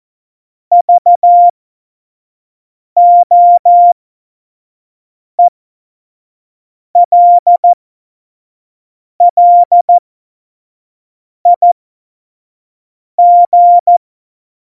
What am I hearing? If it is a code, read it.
Morse code, heard as a keyed tone, about 14 wpm, spaced as for 5 wpm, 700 Hz. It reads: VOELLIG